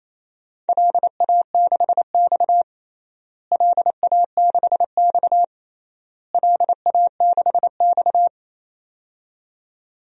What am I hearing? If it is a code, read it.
Morse code, heard as a keyed tone, about 27 wpm, 700 Hz. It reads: LA6X LA6X LA6X
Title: DXCC-100 Call Signs - 4 Characters - Repeated 3x - 28wpm